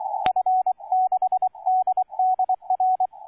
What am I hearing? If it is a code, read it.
Morse code, heard as a keyed tone, about 23 wpm, 750 Hz, with radio noise. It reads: F6DDR